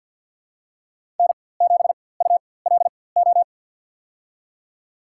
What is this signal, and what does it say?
Morse code, heard as a keyed tone, about 47 wpm, spaced as for 23 wpm, 700 Hz. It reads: N7ULO
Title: USA Call Signs - 5 Characters - 4x Character Spacing - 50wpm